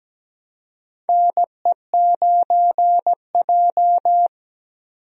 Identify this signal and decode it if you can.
Morse code, heard as a keyed tone, about 17 wpm, 700 Hz. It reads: NE9J
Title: USA Call Signs - 4 Characters - Repeated 3x - 17wpm